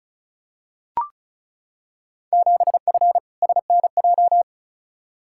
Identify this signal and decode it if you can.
Morse code, heard as a keyed tone, about 36 wpm, 700 Hz. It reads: 7F SNJ
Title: Field Day Exchanges Encore - 35wpm